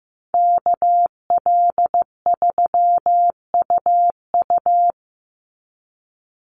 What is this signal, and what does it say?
Morse code, heard as a keyed tone, about 15 wpm, 700 Hz. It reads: KL3UU
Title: USA Call Signs 15wpm